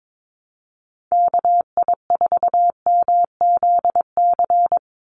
Morse code, heard as a keyed tone, about 22 wpm, 700 Hz. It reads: KI4MZC
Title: Call Signs Repeated 2x - 22wpm